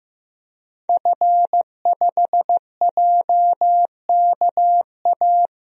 Morse code, heard as a keyed tone, about 15 wpm, 700 Hz. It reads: F5JKA
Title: DXCC-100 Call Signs - 5 Characters - 15wpm